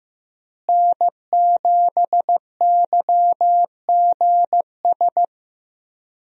Morse code, heard as a keyed tone, about 15 wpm, 700 Hz. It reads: N7YGS
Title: USA Call Signs - 5 Characters - Repeated 2x - 15wpm